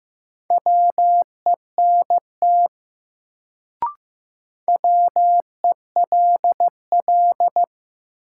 Morse code, heard as a keyed tone, about 15 wpm, 700 Hz. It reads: WENT WELL
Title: Sentences from Top 200 Words - New Word Drill - 15wpm